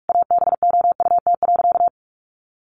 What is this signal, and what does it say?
Morse code, heard as a keyed tone, about 46 wpm, 700 Hz. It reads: ABOUT.
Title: Sentences from Top 400 Words 45wpm